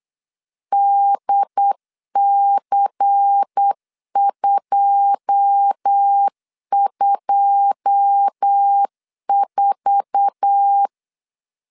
Morse code, heard as a keyed tone, about 8 wpm, 800 Hz. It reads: DC224
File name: LEZIONI / 9 2 8wpm 3437